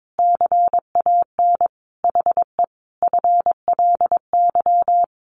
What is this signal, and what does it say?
Morse code, heard as a keyed tone, about 22 wpm, 700 Hz. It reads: CAN HE FLY
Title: Sentences from Top 500 Words 22wpm